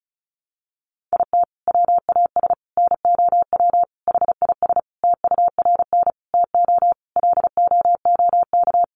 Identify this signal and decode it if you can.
Morse code, heard as a keyed tone, about 34 wpm, 700 Hz. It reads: IT WAS NOW HIS TURN TO LOOK
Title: Sentences from Top 200 Words 35wpm